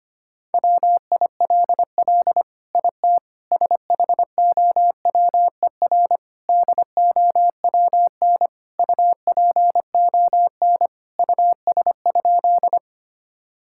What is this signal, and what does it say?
Morse code, heard as a keyed tone, about 25 wpm, 700 Hz. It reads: WILL IT SHOWER DOWN UPON US?
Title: Sentences from Top 1600 Words 25wpm